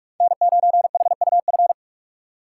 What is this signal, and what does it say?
Morse code, heard as a keyed tone, about 43 wpm, 700 Hz. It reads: N9HUF
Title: USA Call Signs - 5 Characters - 2x Character Spacing - 45wpm